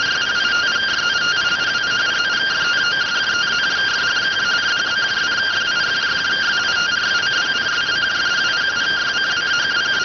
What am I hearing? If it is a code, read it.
Two-tone teleprinter signal, about 170 Hz shift.